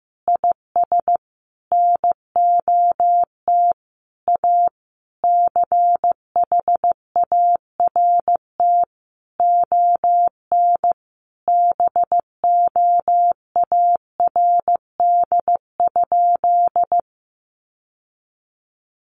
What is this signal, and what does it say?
Morse code, heard as a keyed tone, about 15 wpm, 700 Hz. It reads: IS NOT A CHART ON BOARD?
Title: Sentences from Top 1300 Words 15wpm